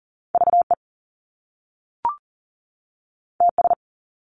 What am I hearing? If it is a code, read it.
Morse code, heard as a keyed tone, about 40 wpm, 700 Hz. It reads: VE TS
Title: Top Two Letter Combinations in Words 40wpm